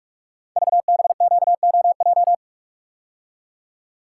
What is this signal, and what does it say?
Morse code, heard as a keyed tone, about 44 wpm, 690 Hz. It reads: VBQOJ
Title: Code Groups - 5 Letters - 45wpm